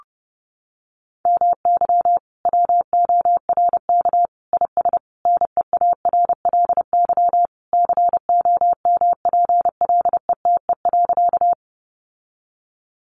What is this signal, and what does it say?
Morse code, heard as a keyed tone, about 30 wpm, 700 Hz. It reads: MY WORK IS NEARLY COMPLETE.
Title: Sentences from Top 1300 Words 30wpm